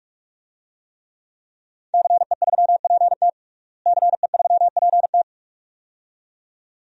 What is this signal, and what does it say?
Morse code, heard as a keyed tone, about 42 wpm, 700 Hz. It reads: CE3PT CE3PT
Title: DXCC-100 Call Signs - 5 Characters - Repeated 2x - 45wpm